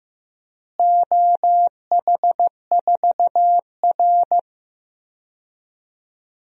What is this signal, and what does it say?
Morse code, heard as a keyed tone, about 15 wpm, 700 Hz. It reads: OH4R
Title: DXCC-100 Call Signs - 4 Characters - Repeated 3x - 15wpm